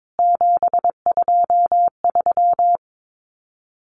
Morse code, heard as a keyed tone, about 22 wpm, 700 Hz. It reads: 723
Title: CWT Exchange 22wpm